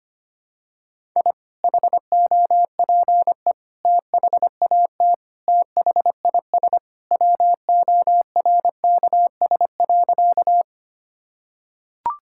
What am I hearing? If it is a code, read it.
Morse code, heard as a keyed tone, about 25 wpm, 700 Hz. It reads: I HOPE THAT THIS WORKS.
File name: practice / 25wpm / Sentences with 2-Word Phrases - 25wpm